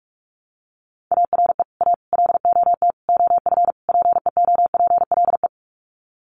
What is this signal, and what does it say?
Morse code, heard as a keyed tone, about 44 wpm, 700 Hz. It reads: ARE A LOT OF PEOPLE
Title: Sets of 5 Words 45wpm